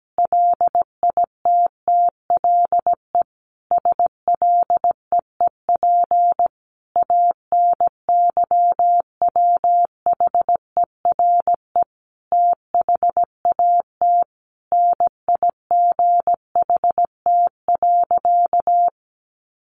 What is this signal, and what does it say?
Morse code, heard as a keyed tone, about 17 wpm, 700 Hz. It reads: LITTLE SLEEP ANYWHERE THAT NIGHT.
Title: Sentences from Top 1400 Words 17wpm